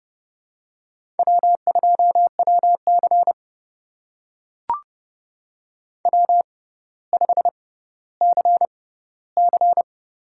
Morse code, heard as a keyed tone, about 31 wpm, spaced as for 16 wpm, 700 Hz. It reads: W2WC W5CC
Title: USA Call Signs - 4 Characters - 6x Character Spacing - 30wpm